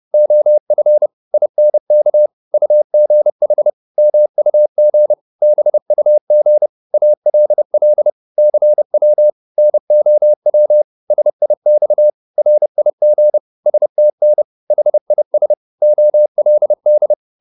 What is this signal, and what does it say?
Morse code, heard as a keyed tone, about 30 wpm, 600 Hz. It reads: OF INK UGH MUG BUG ALL CW NOW SIX RIG STN HIS OLD